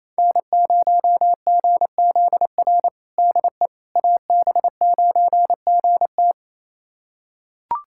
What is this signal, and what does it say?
Morse code, heard as a keyed tone, about 28 wpm, 700 Hz. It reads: N0GZR DE AB9GT
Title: USA Callsign DE Callsign - 5 Characters - 28wpm